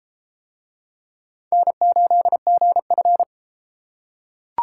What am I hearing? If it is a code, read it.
Morse code, heard as a keyed tone, about 33 wpm, 700 Hz. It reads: N8GF